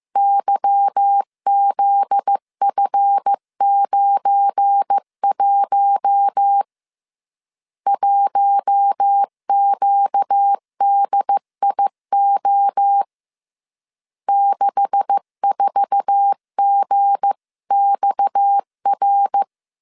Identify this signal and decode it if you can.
Morse code, heard as a keyed tone, about 15 wpm, 800 Hz. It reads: YZF91 1QDIO 64GXR
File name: CW / DET / 1 9 1500C S A 14wpm 3222